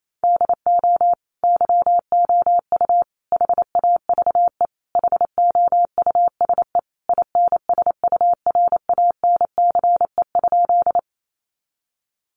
Morse code, heard as a keyed tone, about 28 wpm, 700 Hz. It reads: DO YOU HAVE HOUSE INSURANCE?
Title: Sentences from Top 1100 Words 28wpm